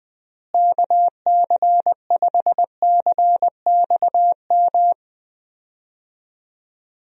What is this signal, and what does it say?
Morse code, heard as a keyed tone, about 20 wpm, 700 Hz. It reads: KC5CXM
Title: USA Call Signs - 6 Characters - Repeated 3x - 20wpm